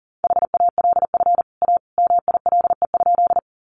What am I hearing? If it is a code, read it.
Morse code, heard as a keyed tone, about 40 wpm, 700 Hz. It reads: HALF A MILE?
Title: Sentences from Top 400 Words 40wpm